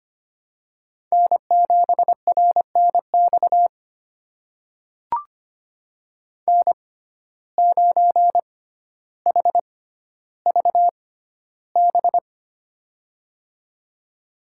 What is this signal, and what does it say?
Morse code, heard as a keyed tone, about 25 wpm, spaced as for 13 wpm, 700 Hz. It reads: N7RNX N9HVB
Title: USA Call Signs - 5 Characters - 6x Character Spacing - 25wpm